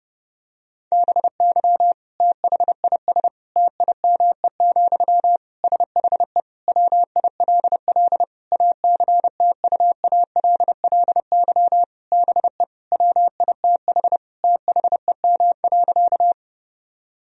Morse code, heard as a keyed tone, about 30 wpm, 700 Hz. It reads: BY THIS TIME, SHE WILL ACTUALLY BE WITH THEM.
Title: Sentences from Top 1000 Words - Review Part 4 - 30wpm